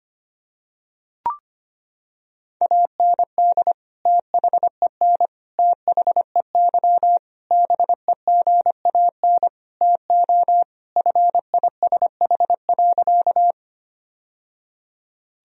Morse code, heard as a keyed tone, about 25 wpm, 700 Hz. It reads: AND THEN THEY BEGAN TO FISH.
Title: Sentences from Top 1000 Words - Review Part 1 - 25wpm